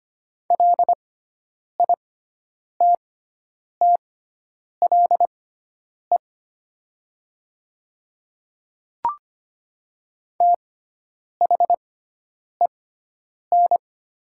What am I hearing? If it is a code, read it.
Morse code, heard as a keyed tone, about 25 wpm, spaced as for 9 wpm, 700 Hz. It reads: LITTLE THEN
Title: 6x Character Spacing - Top 200 Words - 25wpm